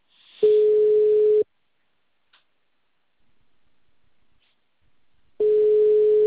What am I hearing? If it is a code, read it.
Ringback tone (Europe, 425 Hz).